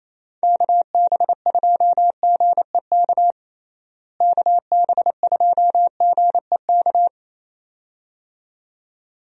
Morse code, heard as a keyed tone, about 28 wpm, 700 Hz. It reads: KB2GEK KB2GEK
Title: USA Call Signs - 6 Characters - Repeated 2x - 28wpm